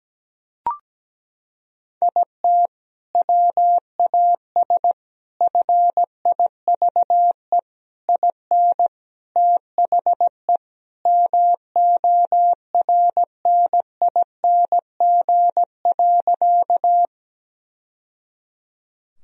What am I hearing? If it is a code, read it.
Morse code, heard as a keyed tone, about 17 wpm, 700 Hz. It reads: IT WAS FIVE IN THE MORNING.
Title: Sentences from Top 1000 Words - Review Part 2 - 17wpm